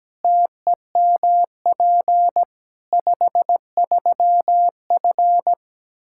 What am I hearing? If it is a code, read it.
Morse code, heard as a keyed tone, about 17 wpm, 700 Hz. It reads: TEMP 53F
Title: Practice QSO - Exchange 2 - Info - 17wpm